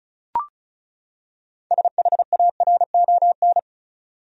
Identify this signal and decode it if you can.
Morse code, heard as a keyed tone, about 36 wpm, 700 Hz. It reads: SHARON
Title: USA Common Names - 35wpm